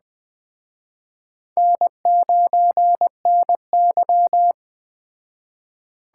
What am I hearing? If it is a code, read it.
Morse code, heard as a keyed tone, about 20 wpm, 700 Hz. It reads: N9NY
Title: Speed Racing - USA Call Signs 17wpm